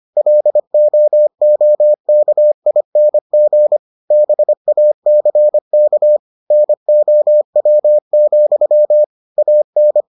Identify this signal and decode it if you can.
Morse code, heard as a keyed tone, about 25 wpm, 600 Hz. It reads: LOOKING BACK NOW, AN